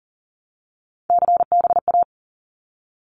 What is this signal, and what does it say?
Morse code, heard as a keyed tone, about 40 wpm, 700 Hz. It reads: CBA
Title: Abbreviations Repeated 3x - 40wpm